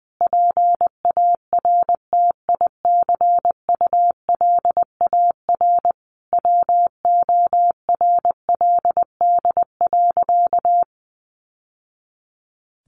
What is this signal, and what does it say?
Morse code, heard as a keyed tone, about 20 wpm, 700 Hz. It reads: PARTICULAR WORLD.
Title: Sentences from Top 800 Words 20wpm